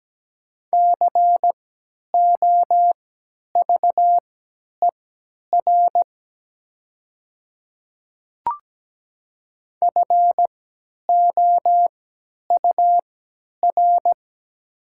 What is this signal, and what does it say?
Morse code, heard as a keyed tone, about 17 wpm, spaced as for 10 wpm, 690 Hz. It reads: COVER FOUR
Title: 3x Character Spacing - Top 300 Words - 17wpm